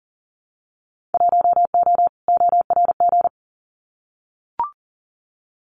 Morse code, heard as a keyed tone, about 40 wpm, 700 Hz. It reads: 1O ORG
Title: Winter Field Day Exchanges 40wpm